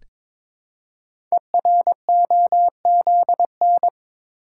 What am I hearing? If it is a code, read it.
Morse code, heard as a keyed tone, about 22 wpm, 700 Hz. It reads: EROZN
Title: Code Groups - 5 Letters - 22wpm